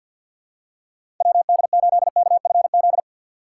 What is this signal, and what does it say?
Morse code, heard as a keyed tone, about 51 wpm, 700 Hz. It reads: WD8KFZ
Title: USA Call Signs - 6 Characters - 4x Character Spacing - 50wpm